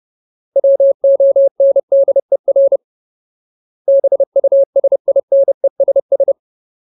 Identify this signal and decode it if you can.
Morse code, heard as a keyed tone, about 30 wpm, 550 Hz. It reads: WONDER BUSINESS
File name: Ultimate Word List / 30WPM Ultimate